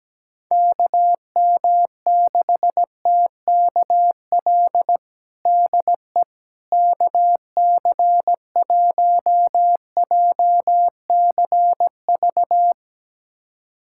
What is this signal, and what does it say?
Morse code, heard as a keyed tone, about 17 wpm, 700 Hz. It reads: KM6TKL DE KC1JCV
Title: USA Callsign DE Callsign 17wpm